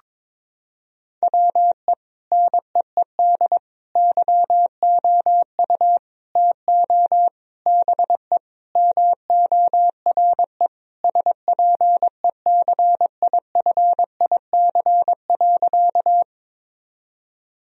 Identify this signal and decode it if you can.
Morse code, heard as a keyed tone, about 22 wpm, 700 Hz. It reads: WE NEED YOU TO BE MORE SPECIFIC.
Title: Sentences from Top 1000 Words - Review Part 3 - 22wpm